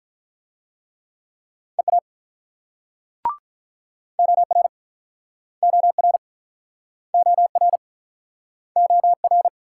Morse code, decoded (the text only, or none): EA OR OR OR OR